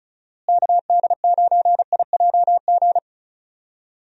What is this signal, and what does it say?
Morse code, heard as a keyed tone, about 34 wpm, 700 Hz. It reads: KD9IJG
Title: USA Call Signs - 6 Characters - Repeated 3x - 35wpm